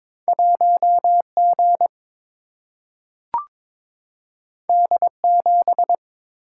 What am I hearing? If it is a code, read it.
Morse code, heard as a keyed tone, about 21 wpm, 700 Hz. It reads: 1G D7
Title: Single-Letter Single-Number 22wpm